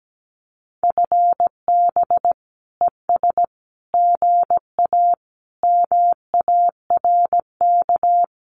Morse code, heard as a keyed tone, about 17 wpm, 700 Hz. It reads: FB ES GA MARK
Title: Common QSO Phrases Encore 17wpm